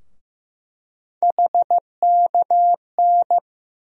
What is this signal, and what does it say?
Morse code, heard as a keyed tone, about 16 wpm, 700 Hz. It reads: HKN
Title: USA Call Sign Postfixes - 3 Letters 15wpm